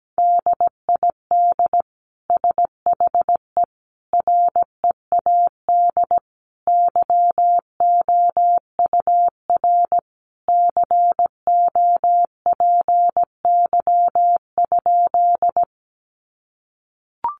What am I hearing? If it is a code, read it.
Morse code, heard as a keyed tone, about 17 wpm, 700 Hz. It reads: DID SHE READ YOUR COPY?
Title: Sentences from Top 1400 Words 17wpm